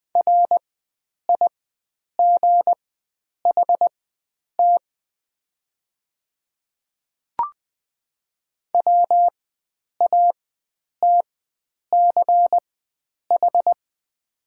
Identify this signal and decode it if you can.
Morse code, heard as a keyed tone, about 21 wpm, spaced as for 9 wpm, 700 Hz. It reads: RIGHT WATCH
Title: 4x Character Spacing - Top 300 Words - 20wpm